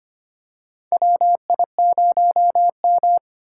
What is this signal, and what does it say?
Morse code, heard as a keyed tone, about 25 wpm, 700 Hz. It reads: WI0M
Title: USA Call Signs - 4 Characters - 4x Character Spacing - 25wpm